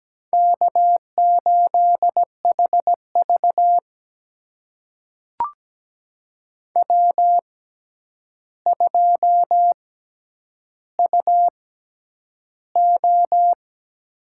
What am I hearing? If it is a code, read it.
Morse code, heard as a keyed tone, about 17 wpm, spaced as for 9 wpm, 700 Hz. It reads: K8HV W2UO
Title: USA Call Signs - 4 Characters - 6x Character Spacing - 17wpm